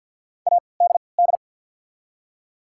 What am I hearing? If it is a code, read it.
Morse code, heard as a keyed tone, about 53 wpm, spaced as for 29 wpm, 700 Hz. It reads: ADD
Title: 3x Character Spacing - Top 200 Words - 50wpm